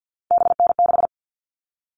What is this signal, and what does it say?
Morse code, heard as a keyed tone, about 50 wpm, 700 Hz. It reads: BN6
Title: All Cut Numbers - 3-Digit 50wpm